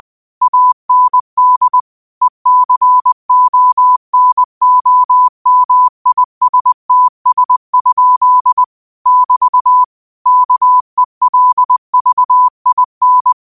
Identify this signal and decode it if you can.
Morse code, heard as a keyed tone, about 20 wpm, 1000 Hz. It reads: AND ECONOMISTS? <BT> KELVIN
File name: 20wpm